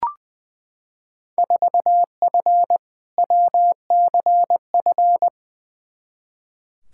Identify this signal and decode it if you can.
Morse code, heard as a keyed tone, about 20 wpm, 700 Hz. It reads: 4F WCF
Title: Field Day Exchanges - 20wpm